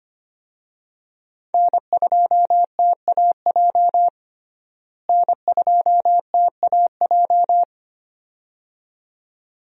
Morse code, heard as a keyed tone, about 25 wpm, 700 Hz. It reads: N2TAJ N2TAJ
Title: USA Call Signs - 5 Characters - Repeated 2x - 25wpm